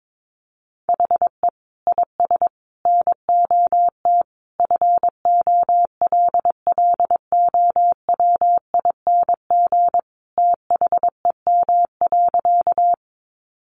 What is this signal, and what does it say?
Morse code, heard as a keyed tone, about 22 wpm, 700 Hz. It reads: HE IS NOT FOLLOWING THEM.
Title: Sentences from Top 700 Words 22wpm